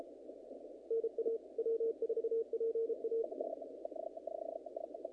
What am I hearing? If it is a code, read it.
Morse code, heard as a keyed tone, about 32 wpm, 460 Hz, with radio noise. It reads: NA W4PA
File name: contests / naqp-cw-k5tr-op-k5pi- -2000